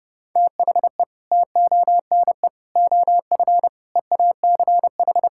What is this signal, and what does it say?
Morse code, heard as a keyed tone, about 32 wpm, 700 Hz. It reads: THE TONE OF EACH
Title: Sentences from Top 1600 Words 30wpm